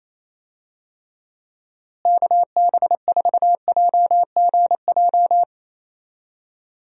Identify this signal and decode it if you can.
Morse code, heard as a keyed tone, about 28 wpm, 700 Hz. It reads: KB4JGJ
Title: USA Call Signs - 6 Characters - Repeated 3x - 28wpm